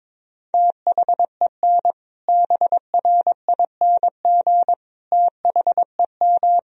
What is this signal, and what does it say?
Morse code, heard as a keyed tone, about 21 wpm, 700 Hz. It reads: THEN BRING THEM
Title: Sentences from Top 500 Words and Abbreviations 22wpm